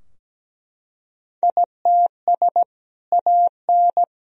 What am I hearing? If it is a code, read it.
Morse code, heard as a keyed tone, about 17 wpm, 700 Hz. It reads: ITS AN
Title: Sentences from Top 1000 Words - Review Part 4 - 17wpm